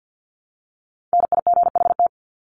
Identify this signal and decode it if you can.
Morse code, heard as a keyed tone, about 50 wpm, 700 Hz. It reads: NIGHT
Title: Top 500 Words 50wpm